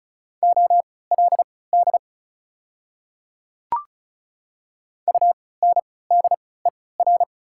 Morse code, heard as a keyed tone, about 34 wpm, spaced as for 20 wpm, 700 Hz. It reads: OLD UNDER
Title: 3x Character Spacing - Top 300 Words - 35wpm